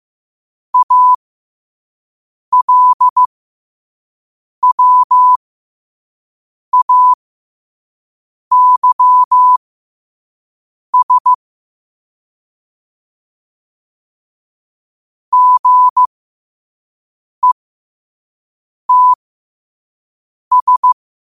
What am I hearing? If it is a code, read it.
Morse code, heard as a keyed tone, about 15 wpm, spaced as for 5 wpm, 1000 Hz. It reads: ALWAYS GETS
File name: qotdfiles / 5wpm